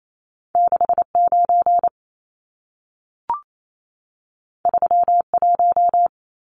Morse code, heard as a keyed tone, about 28 wpm, 700 Hz. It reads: 69 31
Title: Numbers - 2-Digit 28wpm